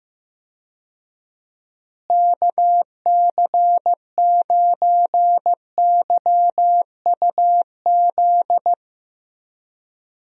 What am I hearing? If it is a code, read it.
Morse code, heard as a keyed tone, about 15 wpm, 700 Hz. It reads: KC9YUZ
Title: USA Call Signs - 6 Characters - Repeated 2x - 15wpm